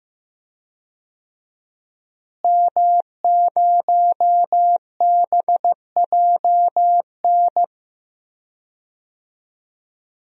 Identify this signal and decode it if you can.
Morse code, heard as a keyed tone, about 15 wpm, 700 Hz. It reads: M0BJN
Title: DXCC-100 Call Signs - 5 Characters - Repeated 3x - 15wpm